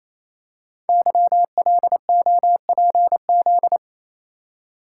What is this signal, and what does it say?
Morse code, heard as a keyed tone, about 29 wpm, 700 Hz. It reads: YLOPZ
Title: Code Groups - 5 Letters - 28wpm